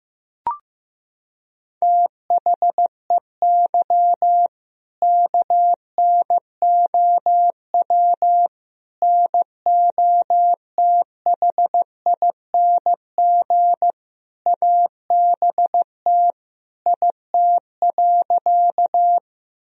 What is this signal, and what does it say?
Morse code, heard as a keyed tone, about 15 wpm, 700 Hz. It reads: THEY KNOW NOTHING ABT IT.
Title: Sentences from Top 500 Words and Abbreviations 15wpm